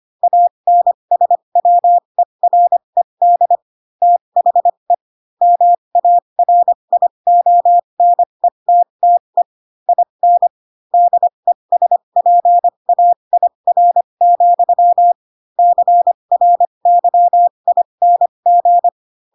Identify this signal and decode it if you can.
Morse code, heard as a keyed tone, about 25 wpm, spaced as for 22 wpm, 700 Hz. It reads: ANSWERED THE MARIONETTE IN DESPAIR, CRYING
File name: Pinocchio / Pinocchio ch7 18 25 ik1hge